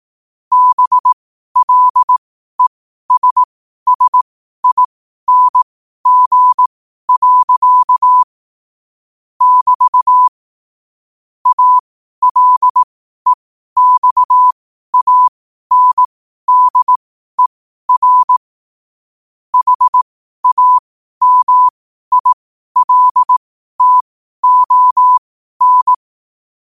Morse code, heard as a keyed tone, about 18 wpm, spaced as for 12 wpm, 1000 Hz. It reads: BLESSING. <BT> ALEXANDER HAMILTON
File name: qotdfiles / 12wpm